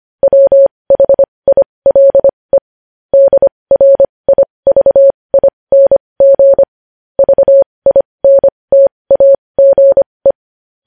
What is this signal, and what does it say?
Morse code, heard as a keyed tone, about 25 wpm, spaced as for 20 wpm, 550 Hz. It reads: WHILE DRIVING VINTAGE
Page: News Headlines in Morse Code at 20 WPM